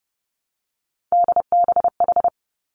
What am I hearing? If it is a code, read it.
Morse code, heard as a keyed tone, about 31 wpm, 700 Hz. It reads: DBH
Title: Three Challenging Letter-Number Combinations 30wpm